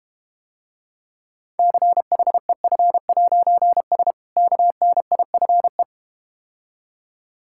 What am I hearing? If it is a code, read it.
Morse code, heard as a keyed tone, about 32 wpm, 700 Hz. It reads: CHEF'S KNIFE